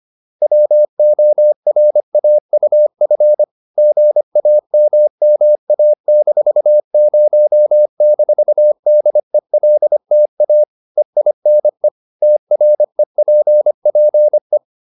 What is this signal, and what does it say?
Morse code, heard as a keyed tone, about 25 wpm, 600 Hz. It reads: WORAUF GAMMA-0-DELTA EINE TREPPE